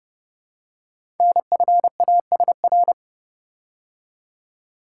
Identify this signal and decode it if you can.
Morse code, heard as a keyed tone, about 30 wpm, 700 Hz. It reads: NFASR
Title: Code Groups - 5 Letters - 30wpm